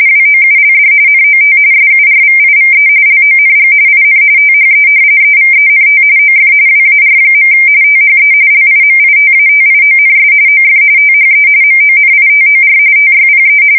RTTY, about 45.45 baud. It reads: WOULD NOT SURVIVE LONG IN THE EVENT OF WAR. THE LIMITATIONS OF THE MONITOR TYPE
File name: fsk170 / usnavy3